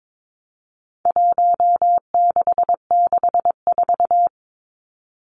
Morse code, heard as a keyed tone, about 22 wpm, 700 Hz. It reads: 1664
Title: Numbers - Contest Style - 4-Digit 22wpm